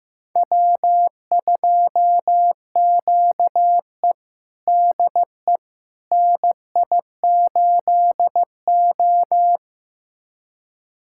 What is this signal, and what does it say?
Morse code, heard as a keyed tone, about 15 wpm, 700 Hz. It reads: W2QE DE NI8O
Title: USA Callsign DE Callsign - 4 Characters - 15wpm